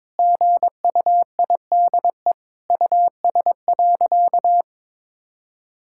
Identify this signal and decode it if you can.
Morse code, heard as a keyed tone, about 22 wpm, 700 Hz. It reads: GUIDE US.
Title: Sentences from Top 1000 Words - Review Part 5 - 22wpm